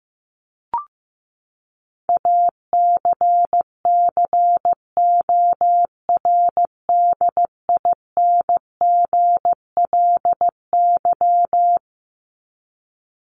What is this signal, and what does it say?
Morse code, heard as a keyed tone, about 15 wpm, 700 Hz. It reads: ACCORDINGLY
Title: LY Suffix - Top 100 Words with Base Words 15wpm